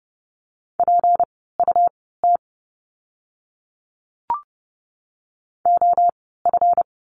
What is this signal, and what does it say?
Morse code, heard as a keyed tone, about 30 wpm, spaced as for 17 wpm, 700 Hz. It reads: PUT OF
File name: course / 30:10wpm / 237 Lesson 38f - Letter q - Words 30wpm